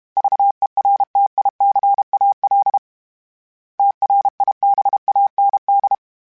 Morse code, heard as a keyed tone, about 32 wpm, 800 Hz. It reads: VERTICAL TRIBAND
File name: fromSMs / FromSMs mp3s / Antennas@30